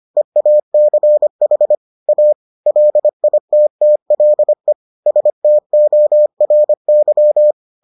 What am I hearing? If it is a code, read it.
Morse code, heard as a keyed tone, about 25 wpm, 600 Hz. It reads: EACH A LITTLE STORY